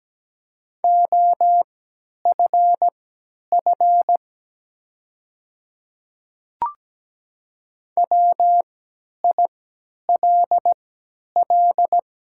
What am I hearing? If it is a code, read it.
Morse code, heard as a keyed tone, about 17 wpm, spaced as for 10 wpm, 700 Hz. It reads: OFF WILL
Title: 3x Character Spacing - Top 300 Words - 17wpm